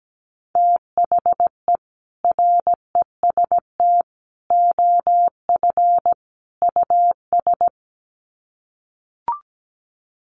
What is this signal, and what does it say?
Morse code, heard as a keyed tone, about 17 wpm, 700 Hz. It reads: THE REST OF US
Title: Sets of 4 Words 17wpm